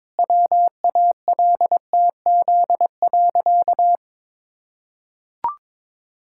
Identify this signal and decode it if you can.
Morse code, heard as a keyed tone, about 22 wpm, 700 Hz. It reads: WALTZ.
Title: Pangram Challenge v2 - 22wpm